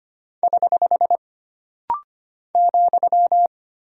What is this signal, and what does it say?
Morse code, heard as a keyed tone, about 25 wpm, 700 Hz. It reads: <HH> ,
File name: Single Punctuation-Prosign - Rapid-Fire - 25wpm